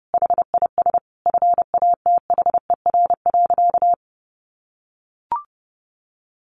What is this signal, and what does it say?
Morse code, heard as a keyed tone, about 30 wpm, 700 Hz. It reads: HIS FATHER.
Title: Sentences from Top 1300 Words 30wpm